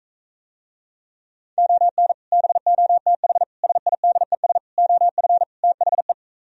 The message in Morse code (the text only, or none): ON BOTH SIDES OF THE